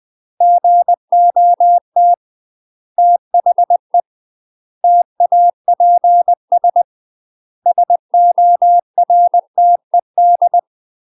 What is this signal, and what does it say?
Morse code, heard as a keyed tone, about 20 wpm, 700 Hz. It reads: GOT THE TAPS SORTED
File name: UK MASTER QSOs TRAINING / 20WPM / UK MASTER QSOs 0026